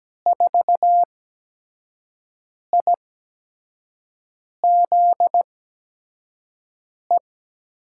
Morse code, heard as a keyed tone, about 17 wpm, spaced as for 5 wpm, 700 Hz. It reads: 4IZE
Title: USA Call Signs - 5 Characters - 8x Character Spacing - 17wpm